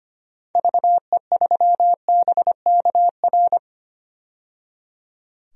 Morse code, heard as a keyed tone, about 25 wpm, 700 Hz. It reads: VE3BKR
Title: DXCC-100 Call Signs - 6 Characters - 25wpm